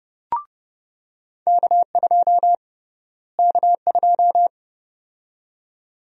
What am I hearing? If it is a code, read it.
Morse code, heard as a keyed tone, about 32 wpm, 700 Hz. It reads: K2 K2
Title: USA Call Sign Prefixes - Repeated 2x - 30wpm